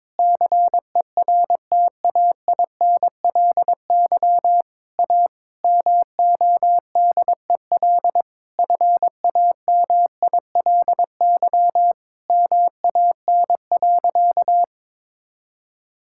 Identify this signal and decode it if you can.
Morse code, heard as a keyed tone, about 22 wpm, 700 Hz. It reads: CERTAINLY A MODEL FAMILY MAN.
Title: Sentences from Top 1000 Words - Review Part 5 - 22wpm